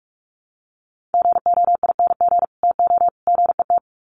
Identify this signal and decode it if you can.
Morse code, heard as a keyed tone, about 44 wpm, 700 Hz. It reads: GOING TO GET
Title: Sets of 3 Words 45wpm